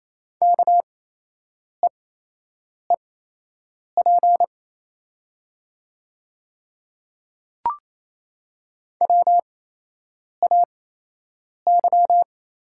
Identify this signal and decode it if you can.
Morse code, heard as a keyed tone, about 28 wpm, spaced as for 8 wpm, 700 Hz. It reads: KEEP WAY
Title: 8x Character Spacing - Top 300 Words - 28wpm